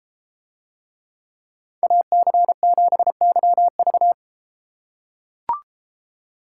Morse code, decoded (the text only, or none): AC7YV